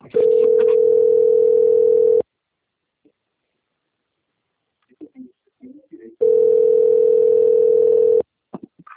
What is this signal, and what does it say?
Ringback tone (North America), 440 + 480 Hz, with speech.
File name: out- - - - - .96413